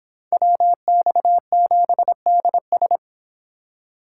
Morse code, heard as a keyed tone, about 29 wpm, 700 Hz. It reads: WX7DS